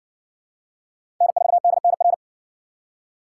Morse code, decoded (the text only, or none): N4DNR